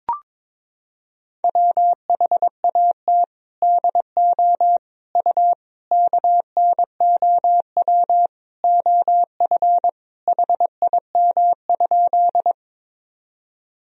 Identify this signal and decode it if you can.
Morse code, heard as a keyed tone, about 22 wpm, 700 Hz. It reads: WHAT DO U KNOW OF HIM?
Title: Sentences from Top 100 Words and Abbreviations 22wpm